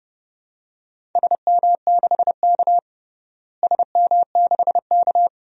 Morse code, heard as a keyed tone, about 30 wpm, 700 Hz. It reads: SM6K SM6K
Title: DXCC-100 Call Signs - 4 Characters - Repeated 2x - 30wpm